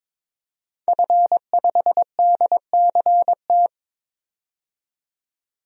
Morse code, heard as a keyed tone, about 22 wpm, 700 Hz. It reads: F5DCT
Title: DXCC-100 Call Signs - 5 Characters - 22wpm